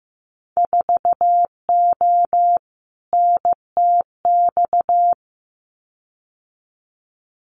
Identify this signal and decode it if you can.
Morse code, heard as a keyed tone, about 15 wpm, 700 Hz. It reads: 4O NTX
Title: Winter Field Day Exchanges 15wpm